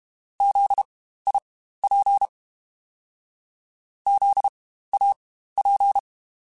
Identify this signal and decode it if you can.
Morse code, heard as a keyed tone, about 32 wpm, spaced as for 15 wpm, 800 Hz. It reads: ZIP ZAP
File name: Spaced Alphas / Spaced Alphas mp3s / Zs sp @30